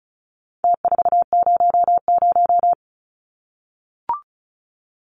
Morse code, heard as a keyed tone, about 34 wpm, 700 Hz. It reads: T400
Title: Numbers - Contest Style - 4-Digit 35wpm